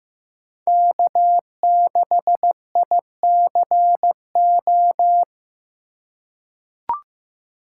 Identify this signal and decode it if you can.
Morse code, heard as a keyed tone, about 15 wpm, 700 Hz. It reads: K6ICO
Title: USA Call Signs - 5 Characters - 4x Character Spacing - 15wpm